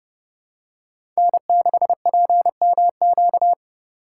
Morse code, heard as a keyed tone, about 31 wpm, 700 Hz. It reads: N6PMQ
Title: USA Call Signs - 3x Character Spacing - 30wpm